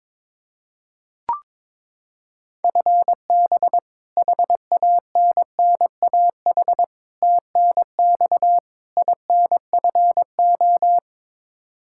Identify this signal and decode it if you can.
Morse code, heard as a keyed tone, about 22 wpm, 700 Hz. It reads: FB HANNAH TNX INFO
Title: Practice QSO - Exchange 3 - More Info - 22wpm